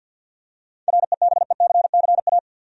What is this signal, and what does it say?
Morse code, heard as a keyed tone, about 47 wpm, 700 Hz. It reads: REBECCA
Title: USA Common Names - 50wpm